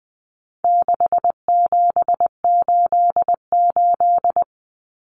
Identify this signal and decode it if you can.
Morse code, heard as a keyed tone, about 20 wpm, 700 Hz. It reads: 6788
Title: Numbers - 4-Digit 20wpm